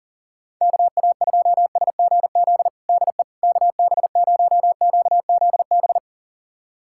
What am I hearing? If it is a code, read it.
Morse code, heard as a keyed tone, about 40 wpm, 700 Hz. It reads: KA2SGZ DE KB0QZB
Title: USA Callsign DE Callsign 40wpm